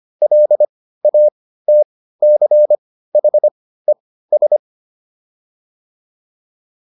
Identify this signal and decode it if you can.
Morse code, heard as a keyed tone, about 25 wpm, spaced as for 15 wpm, 600 Hz. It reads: LATCHES